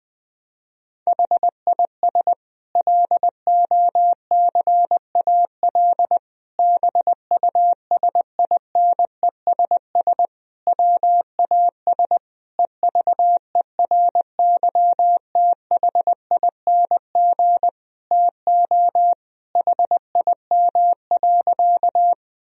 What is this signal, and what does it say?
Morse code, heard as a keyed tone, about 21 wpm, 700 Hz. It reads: HIS LOCAL BUSINESS WAS EVERYTHING TO HIM.
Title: Sentences from Top 1000 Words - Review Part 3 - 20wpm